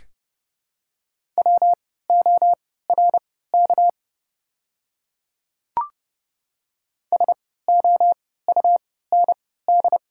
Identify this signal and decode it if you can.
Morse code, heard as a keyed tone, about 30 wpm, spaced as for 17 wpm, 700 Hz. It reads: WORK SOUND
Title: 3x Character Spacing - Top 200 Words - 30wpm